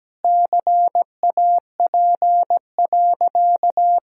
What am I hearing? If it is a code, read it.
Morse code, heard as a keyed tone, about 17 wpm, 700 Hz. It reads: CAP.
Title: Sentences from Top 1600 Words 17wpm